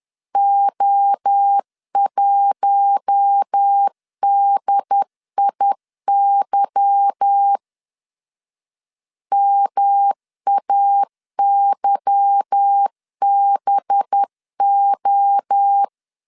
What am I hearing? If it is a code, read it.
Morse code, heard as a keyed tone, about 10 wpm, 800 Hz. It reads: O1DIY MAYBO
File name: LEZIONI / 8 YJ45 / 8 Y 10wpm 2615